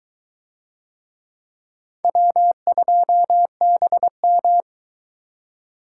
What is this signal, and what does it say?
Morse code, heard as a keyed tone, about 23 wpm, 700 Hz. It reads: W2BM